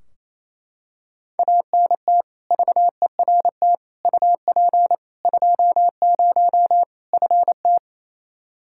Morse code, heard as a keyed tone, about 28 wpm, 700 Hz. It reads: ANT VERT UP 20 FT
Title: K1USN SST QSO Practice 28wpm